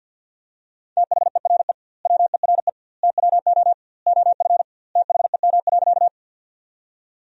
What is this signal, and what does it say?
Morse code, heard as a keyed tone, about 50 wpm, 700 Hz. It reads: THERE WERE TWO OF THEM.
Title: 2x Word-Spacing - Sentences from Top 100 Words - 50wpm